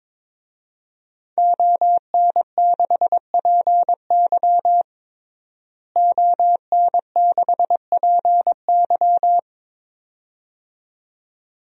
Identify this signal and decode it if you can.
Morse code, heard as a keyed tone, about 23 wpm, 700 Hz. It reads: ON6PY ON6PY
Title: DXCC-100 Call Signs - 5 Characters - Repeated 2x - 22wpm